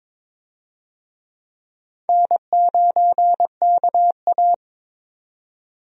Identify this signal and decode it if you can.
Morse code, heard as a keyed tone, about 21 wpm, 700 Hz. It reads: N9KA
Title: USA Call Signs - 4 Characters - 22wpm